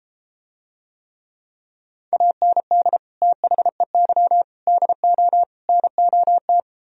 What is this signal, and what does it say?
Morse code, heard as a keyed tone, about 33 wpm, 700 Hz. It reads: AND THEY DO NOT